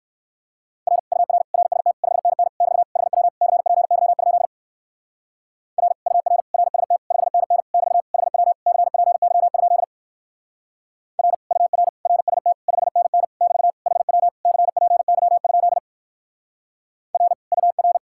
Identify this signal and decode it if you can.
Morse code, heard as a keyed tone, about 57 wpm, 700 Hz. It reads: R UR RST 5NN <BT> HW CPY? R UR RST 5NN <BT> HW CPY? R UR RST 5NN <BT> HW CPY? R UR